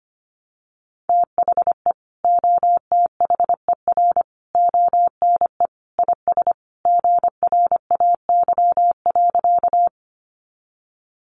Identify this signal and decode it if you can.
Morse code, heard as a keyed tone, about 25 wpm, 700 Hz. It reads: THE OTHER ONE IS GRAY.
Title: Additional Easy Phrases 25wpm